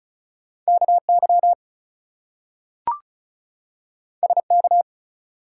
Morse code, heard as a keyed tone, about 36 wpm, 700 Hz. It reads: KY SK
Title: US States and Canadian Provinces 35wpm